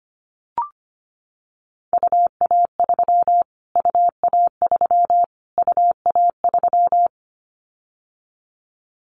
Morse code, heard as a keyed tone, about 25 wpm, 700 Hz. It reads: UA3 UA3 UA3
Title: Callsign Prefixes for Top 25 DXCC - Repeated 3x - 25wpm